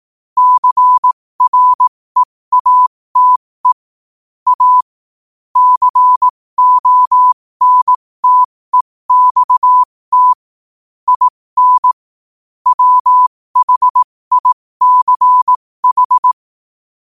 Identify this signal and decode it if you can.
Morse code, heard as a keyed tone, about 18 wpm, spaced as for 15 wpm, 1000 Hz. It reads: CREATE A CONTEXT IN WHICH
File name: qotdfiles / 15wpm